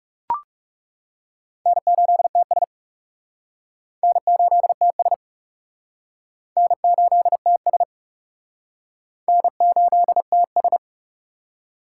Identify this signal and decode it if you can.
Morse code, heard as a keyed tone, about 40 wpm, 700 Hz. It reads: N8TS N8TS N8TS N8TS